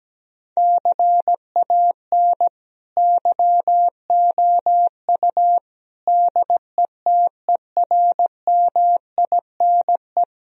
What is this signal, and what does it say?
Morse code, heard as a keyed tone, about 17 wpm, 700 Hz. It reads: CAN YOU DETERMINE
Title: Sentences from Top 1000 Words - Review Part 4 - 17wpm